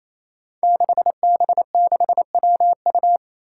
Morse code, heard as a keyed tone, about 28 wpm, 700 Hz. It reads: 6B6WU
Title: Five Challenging Letter-Number Combinations 28wpm